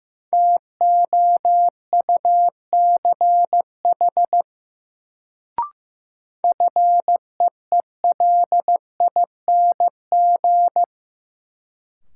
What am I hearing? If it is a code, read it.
Morse code, heard as a keyed tone, about 15 wpm, 700 Hz. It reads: TOUCH FEELING
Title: Sentences from Top 1200 Words - New Word Drill Alt - 15wpm